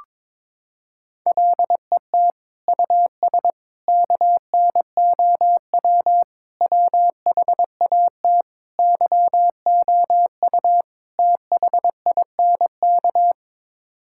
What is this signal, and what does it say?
Morse code, heard as a keyed tone, about 21 wpm, 700 Hz. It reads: LET US KNOW WHAT YOU THINK
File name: practice / 22wpm / Sets of 6 Words - 22wpm